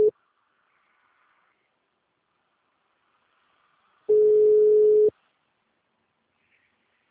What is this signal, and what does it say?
Ringback tone (Europe, 425 Hz).